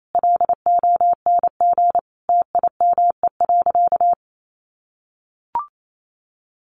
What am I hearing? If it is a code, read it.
Morse code, heard as a keyed tone, about 28 wpm, 700 Hz. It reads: LONG TIME.
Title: Sentences from Top 1000 Words 28wpm